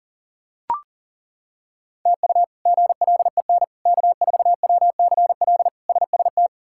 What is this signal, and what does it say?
Morse code, heard as a keyed tone, about 40 wpm, 700 Hz. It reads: TU GLEN K4WCL SST
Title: K1USN SST QSO Practice - 40wpm